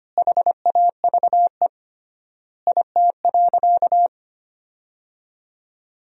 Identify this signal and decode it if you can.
Morse code, heard as a keyed tone, about 25 wpm, 700 Hz. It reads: HAVE IT.
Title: 3x Word-Spacing - Sentences from Top 300 Words - 25wpm